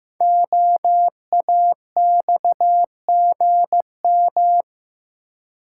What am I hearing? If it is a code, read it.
Morse code, heard as a keyed tone, about 15 wpm, 700 Hz. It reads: OAXGM
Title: Code Groups - 5 Letters - 15wpm